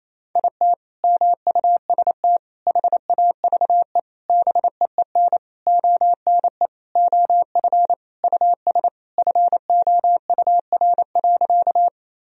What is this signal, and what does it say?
Morse code, heard as a keyed tone, about 28 wpm, 700 Hz. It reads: IT MUST HAVE BEEN ONE OF US FOUR.
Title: Sentences from Top 1000 Words - Review Part 1 - 28wpm